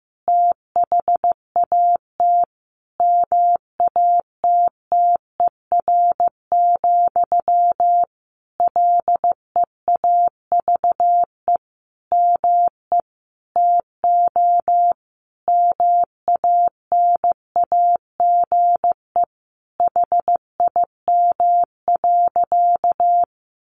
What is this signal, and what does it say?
Morse code, heard as a keyed tone, about 14 wpm, 700 Hz. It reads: THAT MATTER, LEAVE ME TO MANAGE HIM.
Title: Sentences from Top 1400 Words 15wpm